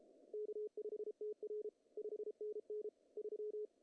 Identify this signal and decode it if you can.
Morse code, heard as a keyed tone, about 33 wpm, 410 Hz. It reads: K5TR 5NN 3
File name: CW / cqww-cw-k5tr